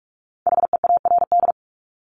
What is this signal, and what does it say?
Morse code, heard as a keyed tone, about 46 wpm, 700 Hz. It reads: HEARD
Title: Multi-letter Phonemes Repeated 3x 45wpm